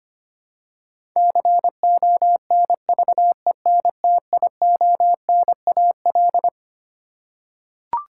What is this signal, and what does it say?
Morse code, heard as a keyed tone, about 25 wpm, 700 Hz. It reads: CONVENTIONAL
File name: Twelve Letter Words - 25wpm